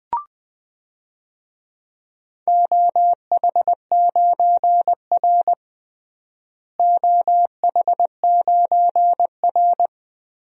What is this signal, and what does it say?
Morse code, heard as a keyed tone, about 21 wpm, 700 Hz. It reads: OH9R OH9R
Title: DXCC-100 Call Signs - 4 Characters - Repeated 3x - 20wpm